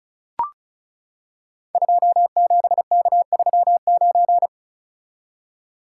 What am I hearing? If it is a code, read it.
Morse code, heard as a keyed tone, about 36 wpm, 700 Hz. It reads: 27K39
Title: Code Groups - 5 Letter:Numbers - 35wpm